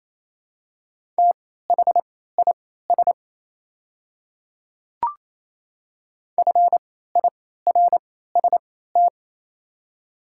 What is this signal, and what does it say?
Morse code, heard as a keyed tone, about 28 wpm, spaced as for 16 wpm, 700 Hz. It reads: THIS FIRST
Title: 3x Character Spacing - Top 100 Words - 28wpm